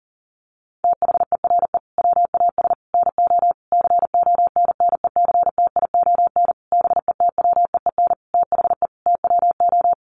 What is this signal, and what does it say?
Morse code, heard as a keyed tone, about 40 wpm, 700 Hz. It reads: THERE WAS NO CONNECTION BETWEEN THE TWO
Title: Sentences from Top 1300 Words 40wpm